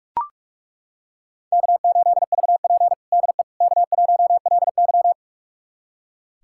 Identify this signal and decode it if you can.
Morse code, heard as a keyed tone, about 42 wpm, 700 Hz. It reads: K8VP DE K1LY
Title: USA Callsign DE Callsign - 4 Characters - 45wpm